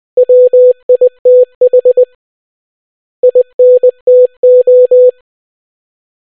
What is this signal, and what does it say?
Morse code, heard as a keyed tone, about 20 wpm, 500 Hz. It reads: WITH INTO